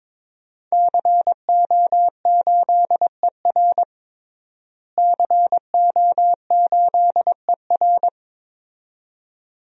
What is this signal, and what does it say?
Morse code, heard as a keyed tone, about 22 wpm, 700 Hz. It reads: CO8ER CO8ER
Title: DXCC-100 Call Signs - 5 Characters - Repeated 2x - 22wpm